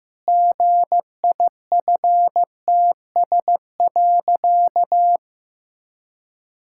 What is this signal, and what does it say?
Morse code, heard as a keyed tone, about 14 wpm, 700 Hz. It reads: GIFTS.
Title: Pangram Challenge v2 - 15wpm